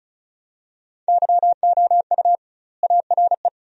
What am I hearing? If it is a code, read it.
Morse code, heard as a keyed tone, about 36 wpm, 700 Hz. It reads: YOU ARE
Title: 2x Word-Spacing - Sentences from Top 400 Words - 35wpm